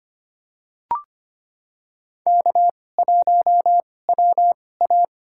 Morse code, heard as a keyed tone, about 25 wpm, spaced as for 18 wpm, 700 Hz. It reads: K1WA
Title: USA Call Signs - 4 Characters - 2x Character Spacing - 25wpm